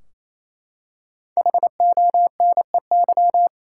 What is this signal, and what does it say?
Morse code, heard as a keyed tone, about 28 wpm, 700 Hz. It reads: HONEY
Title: Five Letter Words - 28wpm